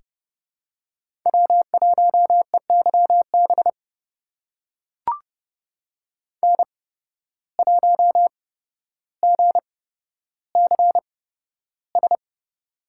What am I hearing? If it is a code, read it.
Morse code, heard as a keyed tone, about 30 wpm, spaced as for 13 wpm, 700 Hz. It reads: W1EYB N1GCS
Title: USA Call Signs - 5 Characters - 8x Character Spacing - 30wpm